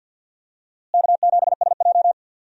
Morse code, heard as a keyed tone, about 51 wpm, 700 Hz. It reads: K7SJ
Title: USA Call Signs - 4 Characters - Repeated 2x - 50wpm